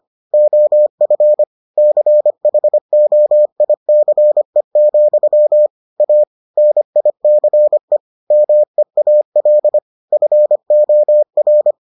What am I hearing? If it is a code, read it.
Morse code, heard as a keyed tone, about 25 wpm, 600 Hz. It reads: OF CHOICE, A NICE MEAL FOR